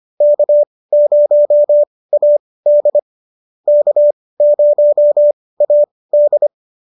Morse code, heard as a keyed tone, about 25 wpm, spaced as for 18 wpm, 590 Hz. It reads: K0AD K0AD